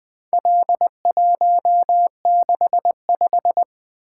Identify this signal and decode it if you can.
Morse code, heard as a keyed tone, about 20 wpm, 700 Hz. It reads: L165
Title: Four Letter-Number Combinations - 20wpm